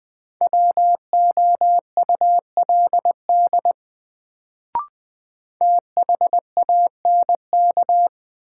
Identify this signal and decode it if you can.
Morse code, heard as a keyed tone, about 20 wpm, 700 Hz. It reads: WOULD THANK
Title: Sets of 2 Words - New Word Drill - 20wpm